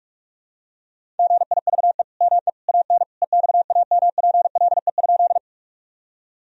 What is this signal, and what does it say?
Morse code, heard as a keyed tone, about 45 wpm, 700 Hz. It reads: GIVE ME AN EXAMPLE?
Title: Sentences from Top 1000 Words - Review Part 1 - 45wpm